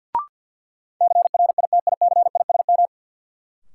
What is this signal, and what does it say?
Morse code, heard as a keyed tone, about 47 wpm, 700 Hz. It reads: CRITICISM
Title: Sentences from Top 1300 Words - New Word Drill - 50wpm